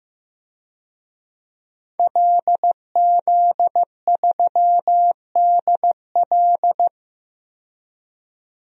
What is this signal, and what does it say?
Morse code, heard as a keyed tone, about 15 wpm, 700 Hz. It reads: LZ3DL
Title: DXCC-100 Call Signs - 5 Characters - 15wpm